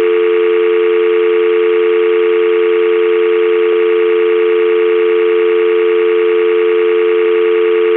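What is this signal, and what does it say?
Dial tone (North America), 350 + 440 Hz, with noise.
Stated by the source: Telephone dial tone sound effect from Zvyki.com.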